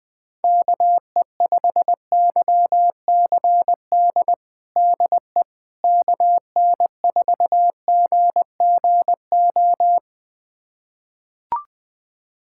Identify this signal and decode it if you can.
Morse code, heard as a keyed tone, about 20 wpm, 700 Hz. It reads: KE5YCD DE KN4GGO
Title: USA Callsign DE Callsign - 6 Characters - 20wpm